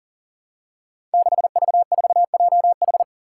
Morse code, heard as a keyed tone, about 40 wpm, 700 Hz. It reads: 6V4JH
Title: Five Challenging Letter-Number Combinations 40wpm